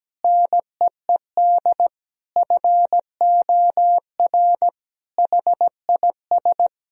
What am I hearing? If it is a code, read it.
Morse code, heard as a keyed tone, about 17 wpm, 700 Hz. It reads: NEED FOR HIS
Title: Sentences from Top 1000 Words - Review Part 3 - 17wpm